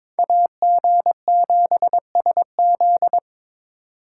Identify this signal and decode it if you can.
Morse code, heard as a keyed tone, about 22 wpm, 700 Hz. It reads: AG7SZ
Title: USA Call Signs - 6x Character Spacing - 22wpm